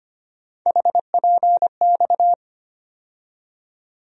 Morse code, heard as a keyed tone, about 25 wpm, 700 Hz. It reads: HPX
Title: USA Call Sign Postfixes - 3 Letters 25wpm